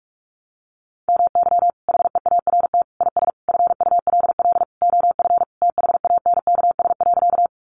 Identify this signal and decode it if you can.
Morse code, heard as a keyed tone, about 44 wpm, 700 Hz. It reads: MY HEART IS FULL OF THANKS.
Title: Sentences from Top 1200 Words 45wpm